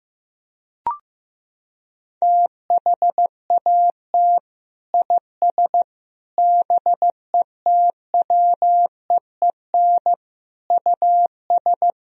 Morse code, heard as a keyed tone, about 15 wpm, 700 Hz. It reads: THAT IS BETWEEN US
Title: Sentences from Top 1000 Words - Review Part 1 - 15wpm